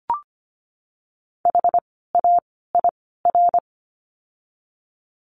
Morse code, heard as a keyed tone, about 25 wpm, 700 Hz. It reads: H A I R
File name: course / 25:10wpm / 094 Lesson 16f - Letter c - Words 25wpm